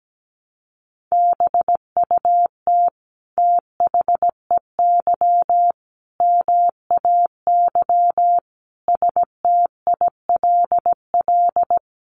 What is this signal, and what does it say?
Morse code, heard as a keyed tone, about 17 wpm, 700 Hz. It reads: BUT THEY MAY STILL
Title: Sentences from Top 300 Words 17wpm